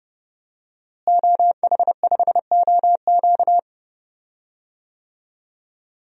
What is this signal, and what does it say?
Morse code, heard as a keyed tone, about 30 wpm, 700 Hz. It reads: OH5OQ
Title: DXCC-100 Call Signs - 5 Characters - Repeated 3x - 30wpm